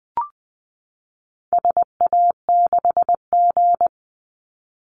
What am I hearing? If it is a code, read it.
Morse code, heard as a keyed tone, about 20 wpm, 700 Hz. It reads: SA6G
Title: Call Signs - 20wpm